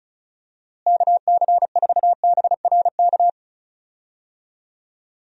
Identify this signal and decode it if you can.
Morse code, heard as a keyed tone, about 36 wpm, 700 Hz. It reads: KC4BRK
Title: USA Call Signs - 6 Characters - Repeated 2x - 35wpm